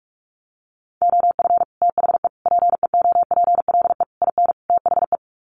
Morse code, heard as a keyed tone, about 44 wpm, 700 Hz. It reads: OF THE PEOPLE IN THE
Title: Sets of 5 Words Encore 45wpm